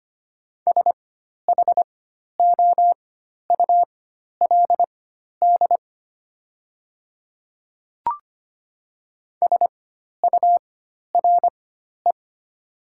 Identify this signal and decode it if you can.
Morse code, heard as a keyed tone, about 25 wpm, spaced as for 12 wpm, 700 Hz. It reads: SHOULD SURE
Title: 4x Character Spacing - Top 300 Words - 25wpm